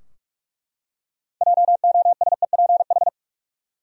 Morse code, heard as a keyed tone, about 47 wpm, 700 Hz. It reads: JOSEPH
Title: USA Common Names - 45wpm